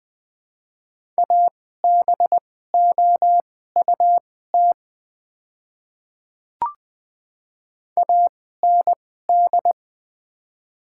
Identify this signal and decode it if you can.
Morse code, heard as a keyed tone, about 21 wpm, spaced as for 15 wpm, 700 Hz. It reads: ABOUT AND
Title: 2x Character Spacing - Top 100 Words - 20wpm